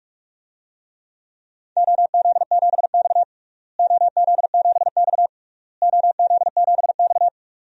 Morse code, heard as a keyed tone, about 41 wpm, 700 Hz. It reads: OZ7X OZ7X OZ7X
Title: DXCC-100 Call Signs - 4 Characters - Repeated 3x - 45wpm